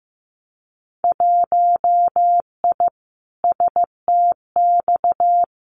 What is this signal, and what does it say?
Morse code, heard as a keyed tone, about 15 wpm, 700 Hz. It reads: 1I STX
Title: Winter Field Day Exchanges 15wpm